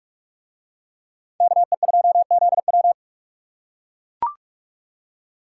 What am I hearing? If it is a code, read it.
Morse code, heard as a keyed tone, about 44 wpm, 700 Hz. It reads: KE2ZW